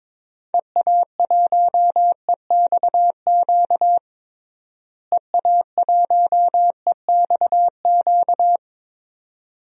Morse code, heard as a keyed tone, about 22 wpm, 700 Hz. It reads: EA1EXQ EA1EXQ
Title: DXCC-100 Call Signs - 6 Characters - Repeated 2x - 22wpm